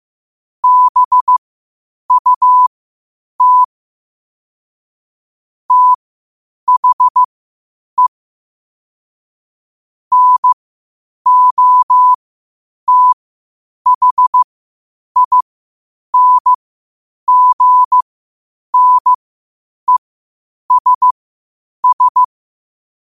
Morse code, heard as a keyed tone, about 15 wpm, spaced as for 8 wpm, 1000 Hz. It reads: BUT THE NOTHINGNESS
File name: qotdfiles / 8wpm